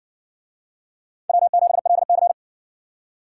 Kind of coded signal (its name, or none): Morse code